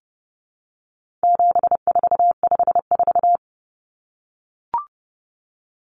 Morse code, heard as a keyed tone, about 30 wpm, 700 Hz. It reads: 7454
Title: Numbers - 4-Digit 30wpm